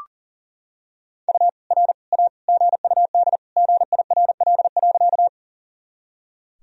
Morse code, heard as a keyed tone, about 40 wpm, 700 Hz. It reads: U R A GUD GIRL.
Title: Sentences from Top 400 Words and Abbreviations 40wpm